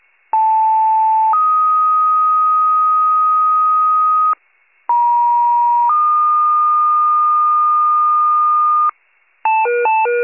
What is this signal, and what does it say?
Two-tone page, heard as a one-second tone, then a three-second tone, about 880 then 1250 Hz, about 950 then 1220 Hz. It is heard from 0.3 to 8.9 s.
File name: pager / 83 RESCUE / wcalert